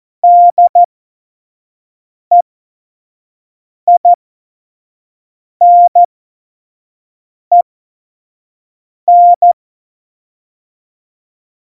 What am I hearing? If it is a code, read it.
Morse code, heard as a keyed tone, about 14 wpm, spaced as for 5 wpm, 700 Hz. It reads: DEINEN
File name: Morsekurs / Witze / Witz 29. August 17, Tempo 25 F70